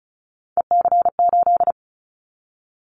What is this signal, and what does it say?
Morse code, heard as a keyed tone, about 34 wpm, 700 Hz. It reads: EC8
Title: Callsign Prefixes for Top 100 DXCC - 35wpm